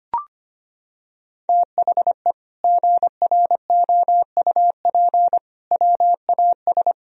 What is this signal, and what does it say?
Morse code, heard as a keyed tone, about 25 wpm, 700 Hz. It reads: THE GROUP WAS
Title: Sentences from Top 1000 Words - Review Part 1 - 25wpm